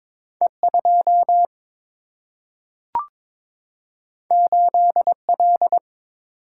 Morse code, heard as a keyed tone, about 22 wpm, 700 Hz. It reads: E2 8L
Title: Single-Letter Single-Number 22wpm